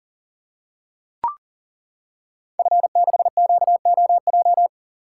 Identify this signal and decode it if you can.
Morse code, heard as a keyed tone, about 38 wpm, 690 Hz. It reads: F6QOJ
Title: Code Groups - 5 Letter:Numbers - 40wpm